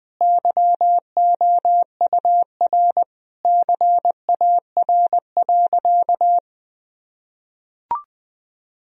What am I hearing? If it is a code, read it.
Morse code, heard as a keyed tone, about 20 wpm, 700 Hz. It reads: YOUR CAR.
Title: Sentences from Top 1000 Words - Review Part 1 - 20wpm